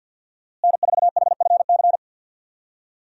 Morse code, heard as a keyed tone, about 50 wpm, 700 Hz. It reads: N4HFC
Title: USA Call Signs - 5 Characters - Repeated 3x - 50wpm